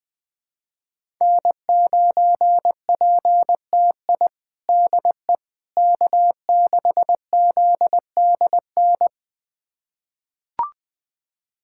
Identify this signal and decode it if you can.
Morse code, heard as a keyed tone, about 20 wpm, 700 Hz. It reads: N9PTI DE K6ZDN
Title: USA Callsign DE Callsign - 5 Characters - 20wpm